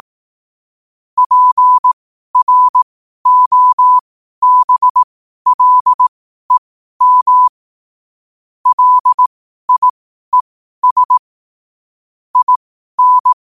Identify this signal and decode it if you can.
Morse code, heard as a keyed tone, about 18 wpm, spaced as for 12 wpm, 1000 Hz. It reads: PROBLEM LIES IN
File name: 12wpm